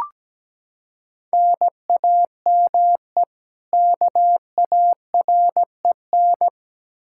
Morse code, heard as a keyed tone, about 17 wpm, 700 Hz. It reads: NAME KAREN
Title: Practice QSO - Exchange 1 - Beginning - 17wpm